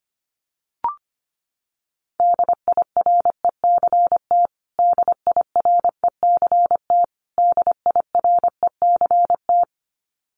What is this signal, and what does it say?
Morse code, heard as a keyed tone, about 25 wpm, 700 Hz. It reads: DIRECT DIRECT DIRECT
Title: Top 500 Words Repeated 25wpm